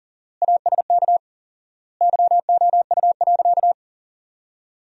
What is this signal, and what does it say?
Morse code, heard as a keyed tone, about 40 wpm, 700 Hz. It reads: ASK YOU.
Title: 4x Word-Spacing - Sentences from Top 400 Words - 40wpm